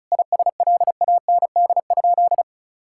Morse code, heard as a keyed tone, about 36 wpm, 700 Hz. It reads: ISLAND?
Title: Sentences from Top 1600 Words 35wpm